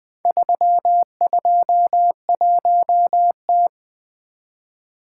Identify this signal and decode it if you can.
Morse code, heard as a keyed tone, about 20 wpm, 700 Hz. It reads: 321T
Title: Four Letter-Number Combinations - 20wpm